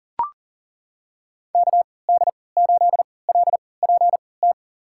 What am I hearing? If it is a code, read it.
Morse code, heard as a keyed tone, about 40 wpm, spaced as for 23 wpm, 700 Hz. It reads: KD8LPT
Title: USA Call Signs - 6 Characters - 3x Character Spacing - 40wpm